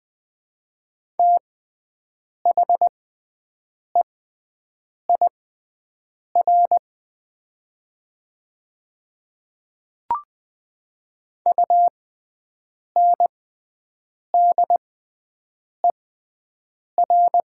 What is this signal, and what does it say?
Morse code, heard as a keyed tone, about 20 wpm, spaced as for 7 wpm, 700 Hz. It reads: THEIR UNDER
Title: 6x Character Spacing - Top 300 Words - 20wpm